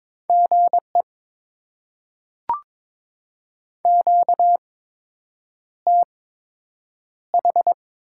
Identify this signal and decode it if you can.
Morse code, heard as a keyed tone, about 22 wpm, spaced as for 8 wpm, 700 Hz. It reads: GE QTH
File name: Top 20 QSO Elements - 8x Character Spacing - 22wpm